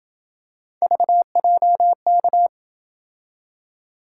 Morse code, heard as a keyed tone, about 28 wpm, 700 Hz. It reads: VJK